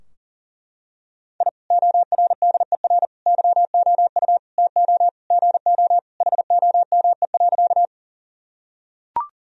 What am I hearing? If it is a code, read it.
Morse code, heard as a keyed tone, about 37 wpm, 700 Hz. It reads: I ORDER YOU TO GO HOME.
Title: Sentences from Top 1000 Words - Review Part 2 - 40wpm